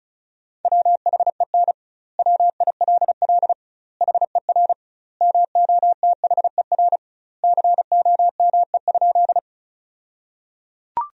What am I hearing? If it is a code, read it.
Morse code, heard as a keyed tone, about 36 wpm, 700 Hz. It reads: WHEN WILL HER MOTHER COME?
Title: 2x Word-Spacing - Sentences from Top 200 Words - 35wpm